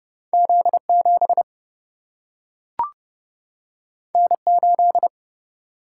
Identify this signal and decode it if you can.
Morse code, heard as a keyed tone, about 29 wpm, 700 Hz. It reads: Z 7 N 8
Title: Single-Letter Single-Number 30wpm